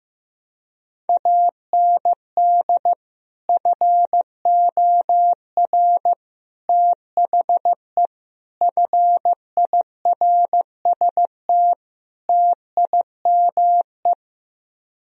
Morse code, heard as a keyed tone, about 15 wpm, 700 Hz. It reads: AND FOR THE FIRST TIME
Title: Speed-Racing - Sets of 5 Words 15wpm